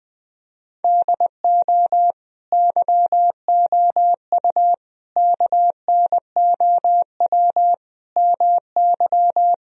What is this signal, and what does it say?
Morse code, heard as a keyed tone, about 20 wpm, 700 Hz. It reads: DO YOU KNOW MY
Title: Sentences from Top 1000 Words - Review Part 3 - 20wpm